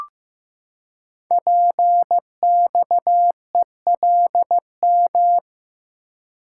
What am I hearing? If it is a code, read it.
Morse code, heard as a keyed tone, about 15 wpm, 700 Hz. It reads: PXELM
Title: Code Groups - 5 Letters - 15wpm